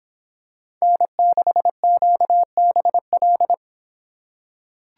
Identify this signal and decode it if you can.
Morse code, heard as a keyed tone, about 26 wpm, 700 Hz. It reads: N6QBL